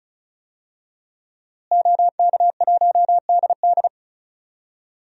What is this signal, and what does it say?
Morse code, heard as a keyed tone, about 36 wpm, 700 Hz. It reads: OK1DD
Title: DXCC-100 Call Signs - 5 Characters - Repeated 2x - 35wpm